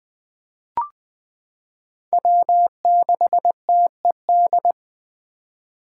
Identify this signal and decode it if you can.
Morse code, heard as a keyed tone, about 20 wpm, 700 Hz. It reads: W6TED
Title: 2023 SOTA Chaser Call Signs - 20wpm